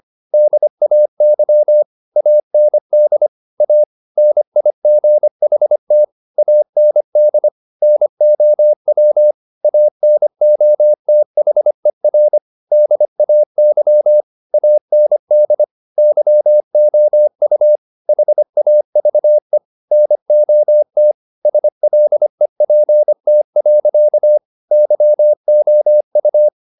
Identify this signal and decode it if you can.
Morse code, heard as a keyed tone, about 25 wpm, 600 Hz. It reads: DAY AND A NIGHT AND NOW ANOTHER DAY AND YOU HAVE NOT SLEPT. YOU